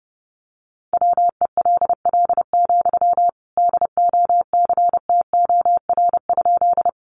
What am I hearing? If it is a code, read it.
Morse code, heard as a keyed tone, about 29 wpm, 700 Hz. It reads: WELL, DOCTOR?
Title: Sentences from Top 1500 Words 30wpm